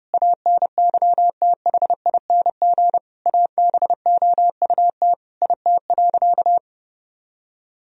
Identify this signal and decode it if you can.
Morse code, heard as a keyed tone, about 30 wpm, 700 Hz. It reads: ANYTHING ABOUT IT.
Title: Sentences from Top 1000 Words - Review Part 2 - 30wpm